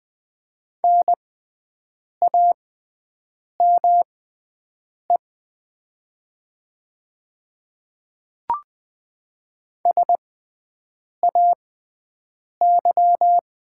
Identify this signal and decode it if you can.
Morse code, heard as a keyed tone, about 20 wpm, spaced as for 7 wpm, 700 Hz. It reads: NAME SAY
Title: 6x Character Spacing - Top 200 Words - 20wpm